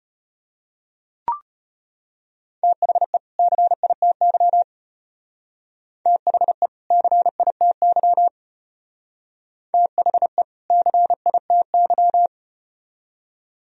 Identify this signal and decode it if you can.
Morse code, heard as a keyed tone, about 35 wpm, 700 Hz. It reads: THE CITY THE CITY THE CITY